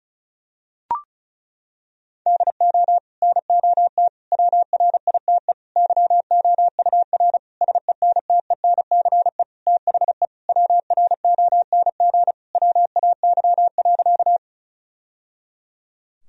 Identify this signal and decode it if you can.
Morse code, heard as a keyed tone, about 36 wpm, 700 Hz. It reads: DO NOT WRITE YOUR SENTENCE THE WRONG WAY.
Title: Sentences from Top 1000 Words - Review Part 1 - 35wpm